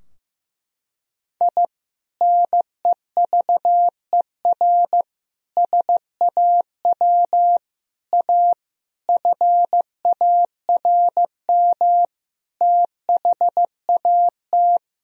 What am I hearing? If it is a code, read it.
Morse code, heard as a keyed tone, about 15 wpm, 700 Hz. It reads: I NEVER SAW A FARM THAT
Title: Sentences from Top 1000 Words - Review Part 2 - 15wpm